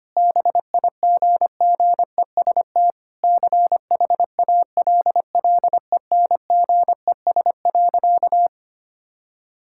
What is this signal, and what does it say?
Morse code, heard as a keyed tone, about 25 wpm, 700 Hz. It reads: BIGGEST CHALLENGES.
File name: practice / 25wpm / Sentences with 4-Word Phrases - 25wpm